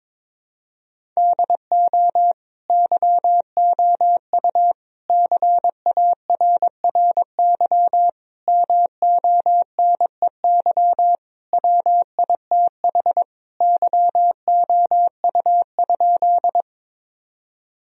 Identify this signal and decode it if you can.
Morse code, heard as a keyed tone, about 22 wpm, 700 Hz. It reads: DO YOU CARRY MONEY WITH YOU?
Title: Sentences from Top 1000 Words - Review Part 1 - 22wpm